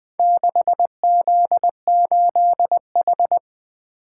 Morse code, heard as a keyed tone, about 20 wpm, 700 Hz. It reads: 6Z8H
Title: Four Letter-Number Combinations - 20wpm